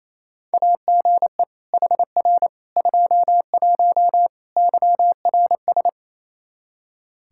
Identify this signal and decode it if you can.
Morse code, heard as a keyed tone, about 28 wpm, 700 Hz. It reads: AGE HR 21 YRS
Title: K1USN SST QSO Practice 28wpm